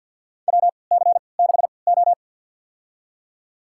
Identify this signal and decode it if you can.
Morse code, heard as a keyed tone, about 49 wpm, spaced as for 28 wpm, 700 Hz. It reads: WC6O
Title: USA Call Signs - 3x Character Spacing - 50wpm